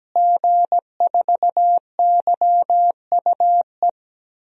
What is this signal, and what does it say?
Morse code, heard as a keyed tone, about 17 wpm, 700 Hz. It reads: G4YUE
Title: DXCC-100 Call Signs - 5 Characters - Repeated 2x - 17wpm